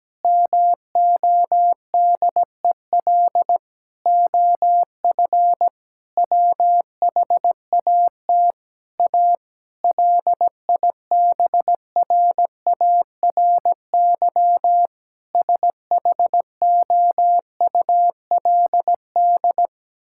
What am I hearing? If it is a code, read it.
Morse code, heard as a keyed tone, about 17 wpm, 700 Hz. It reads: MODEL OF WHAT A LIBRARY SHOULD
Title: Sentences from Top 1000 Words - Review Part 5 - 17wpm